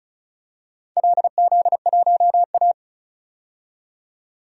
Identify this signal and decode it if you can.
Morse code, heard as a keyed tone, about 36 wpm, 700 Hz. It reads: LZ1A
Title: DXCC-100 Call Signs - 4 Characters - 35wpm